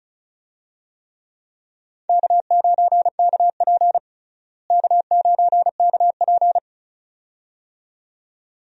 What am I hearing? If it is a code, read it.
Morse code, heard as a keyed tone, about 36 wpm, 700 Hz. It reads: K9KP K9KP
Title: USA Call Signs - 4 Characters - Repeated 2x - 35wpm